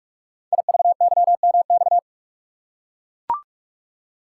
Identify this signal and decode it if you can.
Morse code, heard as a keyed tone, about 44 wpm, 690 Hz. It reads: IVYMX
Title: Code Groups - 5 Letters - 45wpm